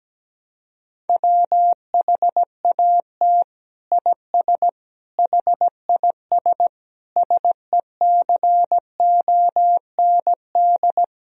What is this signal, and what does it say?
Morse code, heard as a keyed tone, about 17 wpm, 700 Hz. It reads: WHAT IS HIS SECOND
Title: Sentences from Top 1000 Words - Review Part 1 - 17wpm